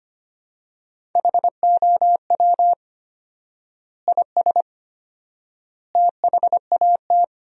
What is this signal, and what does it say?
Morse code, heard as a keyed tone, about 25 wpm, 700 Hz. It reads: HOW IS THAT
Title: 4x Word-Spacing - Sentences from Top 500 Words - 25wpm